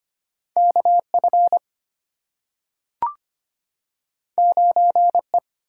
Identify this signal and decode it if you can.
Morse code, heard as a keyed tone, about 25 wpm, 700 Hz. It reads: KF 9E
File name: Two Letter-Number Combinations - 25wpm